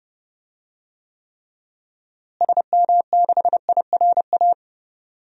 Morse code, heard as a keyed tone, about 30 wpm, 700 Hz. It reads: SM6IRA
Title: DXCC-100 Call Signs - 6 Characters - Repeated 3x - 30wpm